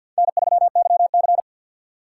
Morse code, heard as a keyed tone, about 50 wpm, 700 Hz. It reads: N3YC
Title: USA Call Signs - 4x Character Spacing - 50wpm